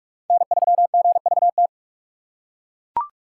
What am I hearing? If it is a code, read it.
Morse code, heard as a keyed tone, about 42 wpm, 700 Hz. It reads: N3GVT
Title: USA Call Signs - 5 Characters - 6x Character Spacing - 45wpm